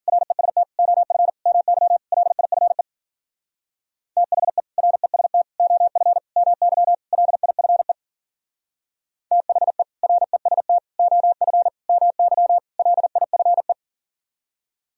Morse code, heard as a keyed tone, about 50 wpm, 700 Hz. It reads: REST OF MY LIFE THE REST OF MY LIFE THE REST OF MY LIFE
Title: Speed-Racing - Sets of 5 Words 40wpm